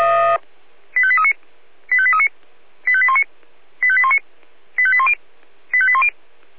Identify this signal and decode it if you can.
Five-tone selective call (ZVEI), heard as quick runs of five tones, tones of about 70 ms. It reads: 85839 85839 85819 85819 85810 85810